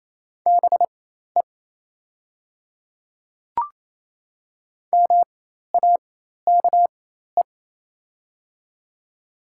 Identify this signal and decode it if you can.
Morse code, heard as a keyed tone, about 28 wpm, spaced as for 13 wpm, 700 Hz. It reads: BE MAKE
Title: 4x Character Spacing - Top 100 Words - 28wpm